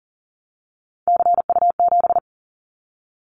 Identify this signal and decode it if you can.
Morse code, heard as a keyed tone, about 42 wpm, 700 Hz. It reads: CU7
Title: Callsign Prefixes for Top 50 DXCC - 40wpm